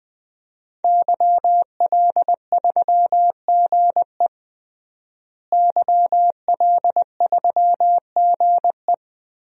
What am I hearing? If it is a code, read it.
Morse code, heard as a keyed tone, about 20 wpm, 700 Hz. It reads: YL3GE YL3GE
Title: DXCC-100 Call Signs - 5 Characters - Repeated 3x - 20wpm